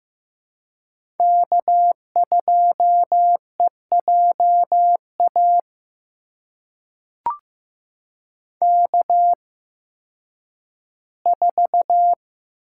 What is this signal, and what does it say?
Morse code, heard as a keyed tone, about 15 wpm, spaced as for 10 wpm, 700 Hz. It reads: K2EJA K4
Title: USA Call Signs - 5 Characters - 8x Character Spacing - 15wpm